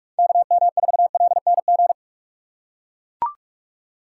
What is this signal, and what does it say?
Morse code, heard as a keyed tone, about 43 wpm, 700 Hz. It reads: KM4LNG
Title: USA Call Signs - 6 Characters - 3x Character Spacing - 45wpm